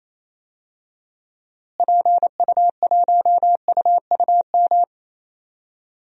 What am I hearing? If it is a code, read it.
Morse code, heard as a keyed tone, about 27 wpm, 700 Hz. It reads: PU1UUM
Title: DXCC-100 Call Signs - 6 Characters - Repeated 2x - 28wpm